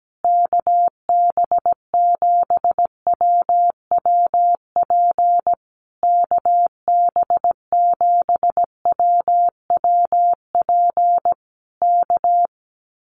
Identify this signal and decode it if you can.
Morse code, heard as a keyed tone, about 17 wpm, 700 Hz. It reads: KB7WWP KB7WWP K
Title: Common QSO Phrases Encore 17wpm